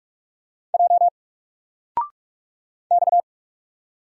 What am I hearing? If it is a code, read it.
Morse code, heard as a keyed tone, about 45 wpm, 700 Hz. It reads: J X
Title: Single Letter-Number 45wpm